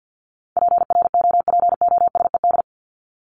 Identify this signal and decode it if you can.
Morse code, heard as a keyed tone, about 50 wpm, 700 Hz. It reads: PROPOSED
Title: ED Suffix - Top 100 Words with Base Words 50wpm